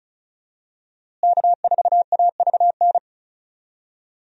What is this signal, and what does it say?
Morse code, heard as a keyed tone, about 36 wpm, 700 Hz. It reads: K4AVN
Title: USA Call Signs - 5 Characters - Repeated 3x - 35wpm